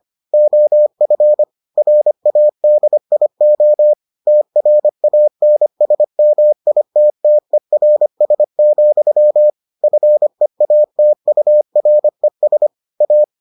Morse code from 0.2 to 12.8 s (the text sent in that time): OF RADIO TRANSMITTERS, FEATURES